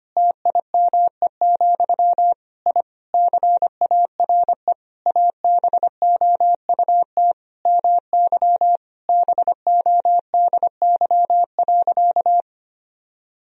Morse code, heard as a keyed tone, about 25 wpm, 700 Hz. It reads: TIME, I CARE ABOUT MY BODY.
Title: Sentences from Top 1000 Words - Review Part 2 - 25wpm